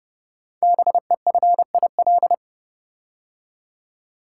Morse code, heard as a keyed tone, about 30 wpm, 700 Hz. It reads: BEFIL
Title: Code Groups - 5 Letters - 30wpm